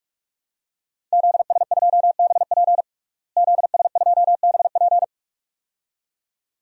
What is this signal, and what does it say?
Morse code, heard as a keyed tone, about 46 wpm, 700 Hz. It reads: ZS2BP ZS2BP
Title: DXCC-100 Call Signs - 5 Characters - Repeated 2x - 45wpm